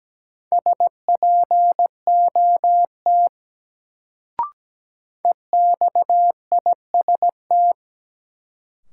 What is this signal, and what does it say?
Morse code, heard as a keyed tone, about 17 wpm, 700 Hz. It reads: SPOT EXIST
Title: Sentences from Top 1100 Words - New Word Drill Alt - 17wpm